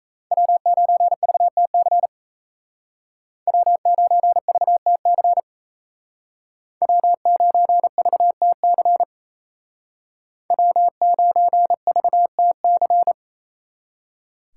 Morse code, heard as a keyed tone, about 34 wpm, 700 Hz. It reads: W9VTC W9VTC W9VTC W9VTC
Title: Speed Racing - USA 1x3 Call Signs 28wpm